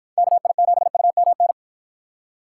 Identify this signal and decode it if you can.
Morse code, heard as a keyed tone, about 53 wpm, 700 Hz. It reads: KI7UMN